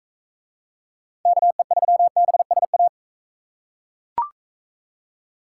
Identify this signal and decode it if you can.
Morse code, heard as a keyed tone, about 42 wpm, 700 Hz. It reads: KE3BSA